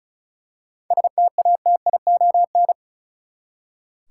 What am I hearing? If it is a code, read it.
Morse code, heard as a keyed tone, about 36 wpm, 700 Hz. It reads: STATION
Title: Seven Letter Words - 35wpm